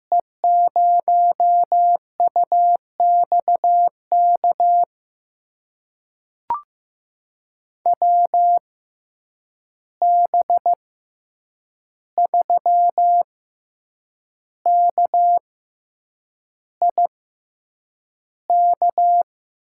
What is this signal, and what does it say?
Morse code, heard as a keyed tone, about 15 wpm, spaced as for 7 wpm, 700 Hz. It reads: E0UXK WB3KIK
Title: USA Call Signs - 6 Characters - 6x Character Spacing - 15wpm